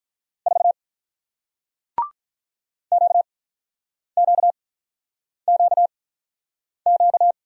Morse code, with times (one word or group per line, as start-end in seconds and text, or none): 0.5-0.7 4
2.9-7.3 QQQQ